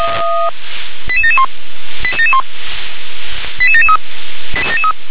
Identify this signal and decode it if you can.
Five-tone selective call (ZVEI), heard as quick runs of five tones, tones of about 70 ms. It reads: 88771 88773
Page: SDR/ZVEI